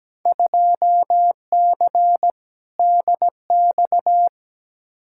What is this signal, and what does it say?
Morse code, heard as a keyed tone, about 17 wpm, 700 Hz. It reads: 2C DX
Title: Field Day Exchanges Encore - 17wpm